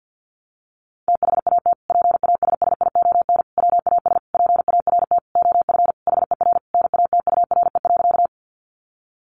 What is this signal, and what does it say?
Morse code, heard as a keyed tone, about 50 wpm, 700 Hz. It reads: THAT PASSION WAS PART OF HER NATURE.
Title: Sentences from Top 1300 Words 50wpm